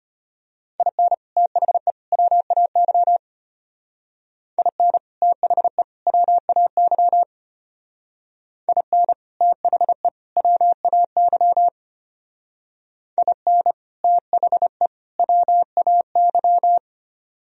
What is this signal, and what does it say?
Morse code, heard as a keyed tone, about 33 wpm, 700 Hz. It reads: IN THE WAY IN THE WAY IN THE WAY IN THE WAY